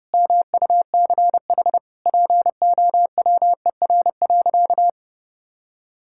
Morse code, heard as a keyed tone, about 32 wpm, 700 Hz. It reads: MUCH POWER.
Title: Sentences from Top 1000 Words - Review Part 5 - 30wpm